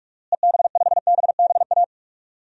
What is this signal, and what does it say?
Morse code, heard as a keyed tone, about 46 wpm, 700 Hz. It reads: EB5BBA
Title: DXCC-100 Call Signs - 6 Characters - Repeated 3x - 45wpm